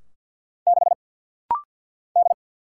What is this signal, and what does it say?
Morse code, heard as a keyed tone, about 48 wpm, 700 Hz. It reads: 6 D
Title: Single Letter-Number - Rapid-Fire 50wpm